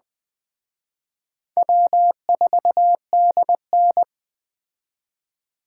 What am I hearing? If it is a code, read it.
Morse code, heard as a keyed tone, about 20 wpm, 700 Hz. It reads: W4DN
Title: Speed Racing - USA 1x2 Call Signs 17wpm